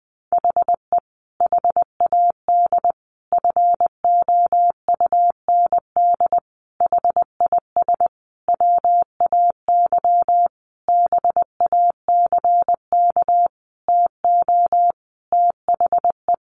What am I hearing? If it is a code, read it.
Morse code, heard as a keyed tone, about 20 wpm, 700 Hz. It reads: HE HAD FOUND HIS WAY BACK TO THE
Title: A Sound Of Thunder 20wpm